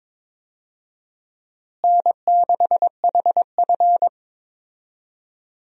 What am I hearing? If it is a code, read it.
Morse code, heard as a keyed tone, about 22 wpm, 700 Hz. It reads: N6HF
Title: USA Call Signs - 4 Characters - Repeated 2x - 22wpm